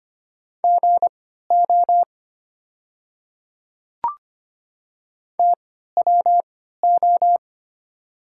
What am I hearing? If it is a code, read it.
Morse code, heard as a keyed tone, about 25 wpm, spaced as for 14 wpm, 700 Hz. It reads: GO TWO
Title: 3x Character Spacing - Top 200 Words - 25wpm